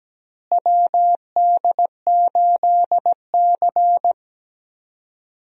Morse code, heard as a keyed tone, about 17 wpm, 700 Hz. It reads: WD8C
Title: USA Call Signs - 4 Characters - Repeated 2x - 17wpm